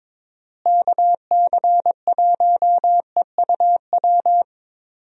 Morse code, heard as a keyed tone, about 21 wpm, 700 Hz. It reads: KC1EUW
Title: USA Call Signs - 6 Characters - Repeated 3x - 22wpm